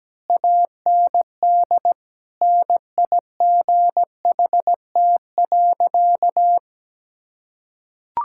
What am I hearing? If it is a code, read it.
Morse code, heard as a keyed tone, about 17 wpm, 700 Hz. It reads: AND NIGHT.
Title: Sentences from Top 1000 Words - Review Part 6 - 17wpm